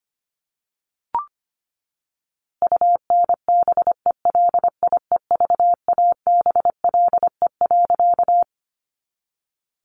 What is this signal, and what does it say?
Morse code, heard as a keyed tone, about 25 wpm, 700 Hz. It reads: UNBELIEVABLE.
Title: A Sound Of Thunder 25wpm